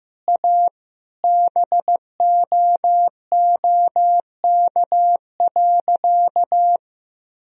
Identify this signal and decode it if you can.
Morse code, heard as a keyed tone, about 15 wpm, 700 Hz. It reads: A BOOK.
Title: Sentences from Top 1000 Words - Review Part 3 - 15wpm